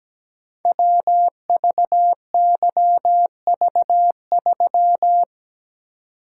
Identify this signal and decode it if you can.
Morse code, heard as a keyed tone, about 17 wpm, 700 Hz. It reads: WVYV3
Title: Code Groups - 5 Letter:Numbers - 17wpm